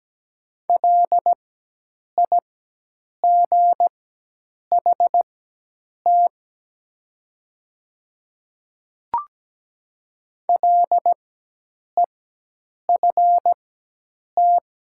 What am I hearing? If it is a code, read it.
Morse code, heard as a keyed tone, about 17 wpm, spaced as for 8 wpm, 700 Hz. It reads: LIGHT LEFT
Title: 4x Character Spacing - Top 300 Words - 17wpm